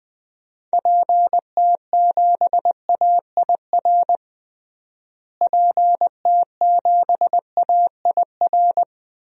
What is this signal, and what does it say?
Morse code, heard as a keyed tone, about 21 wpm, 700 Hz. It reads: PT7AIR PT7AIR
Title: DXCC-100 Call Signs - 6 Characters - Repeated 2x - 20wpm